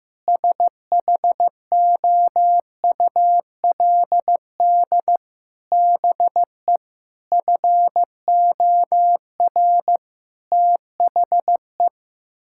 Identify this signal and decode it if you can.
Morse code, heard as a keyed tone, about 16 wpm, 700 Hz. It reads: SHOULD BE FOR THE
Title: Sentences from Top 1000 Words - Review Part 6 - 15wpm